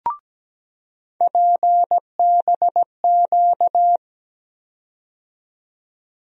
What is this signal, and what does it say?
Morse code, heard as a keyed tone, about 17 wpm, 700 Hz. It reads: PBQ
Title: USA Call Sign Postfixes - 3 Letters 17wpm